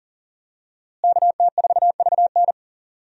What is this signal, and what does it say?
Morse code, heard as a keyed tone, about 40 wpm, 700 Hz. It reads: KT4VN
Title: USA Call Signs - 5 Characters - Repeated 2x - 40wpm